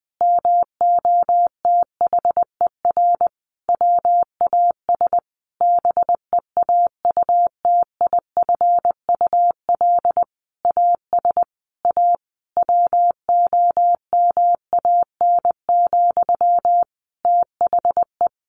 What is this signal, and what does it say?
Morse code, heard as a keyed tone, about 20 wpm, 700 Hz. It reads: MOTHER WAS BEAUTIFUL AS A WOMAN, THE
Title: Sentences from Top 1500 Words 20wpm